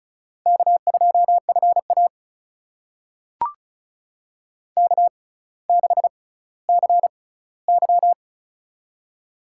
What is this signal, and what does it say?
Morse code, heard as a keyed tone, about 36 wpm, spaced as for 18 wpm, 700 Hz. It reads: K2FA K6CY
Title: USA Call Signs - 4 Characters - 6x Character Spacing - 35wpm